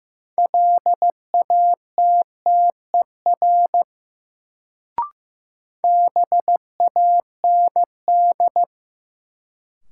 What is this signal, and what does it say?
Morse code, heard as a keyed tone, about 15 wpm, 700 Hz. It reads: LATTER BAND
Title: Sentences from Top 1600 Words - New Word Drill - 15wpm